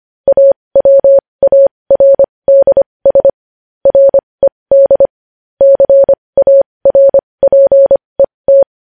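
Morse code, heard as a keyed tone, about 25 wpm, spaced as for 20 wpm, 550 Hz. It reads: AWARDS RED CARPET
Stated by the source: Morse code news headlines practice at 20.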